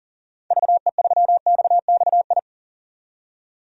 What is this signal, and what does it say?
Morse code, heard as a keyed tone, about 40 wpm, 700 Hz. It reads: VE3XXI
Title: DXCC-100 Call Signs - 6 Characters - 40wpm